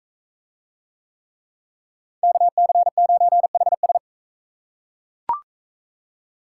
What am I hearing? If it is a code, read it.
Morse code, heard as a keyed tone, about 44 wpm, 700 Hz. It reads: KC9HS